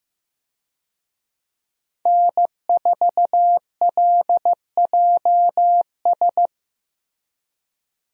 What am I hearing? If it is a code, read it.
Morse code, heard as a keyed tone, about 15 wpm, 700 Hz. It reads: N4LJS
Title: USA Call Signs - 5 Characters - Repeated 3x - 15wpm